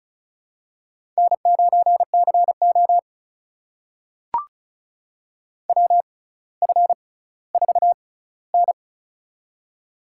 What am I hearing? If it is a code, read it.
Morse code, heard as a keyed tone, about 36 wpm, spaced as for 18 wpm, 700 Hz. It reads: N9CO WF4N
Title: USA Call Signs - 4 Characters - 6x Character Spacing - 35wpm